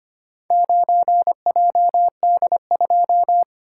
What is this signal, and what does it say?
Morse code, heard as a keyed tone, about 25 wpm, 690 Hz. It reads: 9JD2
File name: practice / 25wpm / Four Letter-Number Combinations - 25wpm